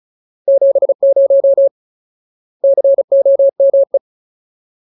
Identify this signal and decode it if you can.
Morse code, heard as a keyed tone, about 34 wpm, 550 Hz. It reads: 70 COME
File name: Ultimate Word List / 35WPM Ultimate